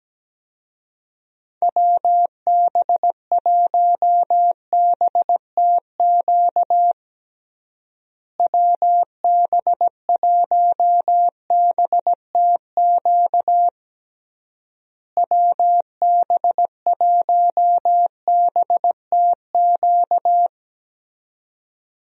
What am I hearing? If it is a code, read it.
Morse code, heard as a keyed tone, about 17 wpm, 700 Hz. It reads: WB1BTQ WB1BTQ WB1BTQ
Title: USA Call Signs - 6 Characters - Repeated 3x - 17wpm